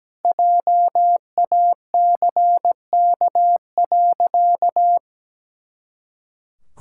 Morse code, heard as a keyed tone, about 17 wpm, 700 Hz. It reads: JACK.
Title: Pangram Challenge v2 - 17wpm